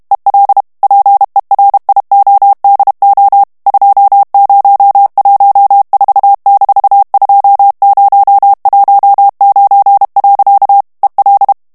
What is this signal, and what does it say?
Morse code, heard as a keyed tone, about 32 wpm, 800 Hz. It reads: EL PERIODO 2014-2019. EL
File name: morse mp3 / ESPAGNOL TEXTE EN CLAIR 1 VITESSE 30 MOTS MINUTES